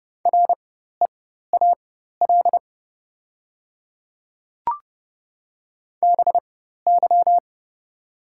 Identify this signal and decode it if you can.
Morse code, heard as a keyed tone, about 30 wpm, spaced as for 14 wpm, 690 Hz. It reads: REAL BY
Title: 4x Character Spacing - Top 300 Words - 30wpm